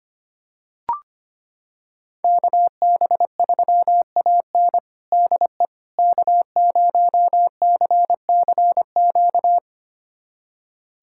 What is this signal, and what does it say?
Morse code, heard as a keyed tone, about 25 wpm, 700 Hz. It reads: KB3AN DE K0CCQ
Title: USA Callsign DE Callsign - 5 Characters - 25wpm